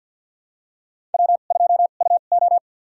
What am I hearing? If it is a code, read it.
Morse code, heard as a keyed tone, about 52 wpm, spaced as for 36 wpm, 700 Hz. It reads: W2UO
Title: USA Call Signs - 2x Character Spacing - 50wpm